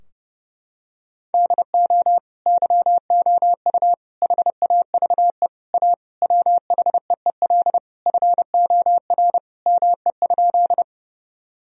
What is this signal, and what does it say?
Morse code, heard as a keyed tone, about 30 wpm, 700 Hz. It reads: DO YOU HAVE A WHEEL FOR ME?
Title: Sentences from Top 1000 Words - Review Part 3 - 30wpm